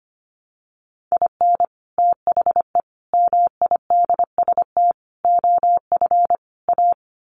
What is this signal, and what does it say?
Morse code, heard as a keyed tone, about 25 wpm, 700 Hz. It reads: IN THE MIDST OF A
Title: Sets of 5 Words 25wpm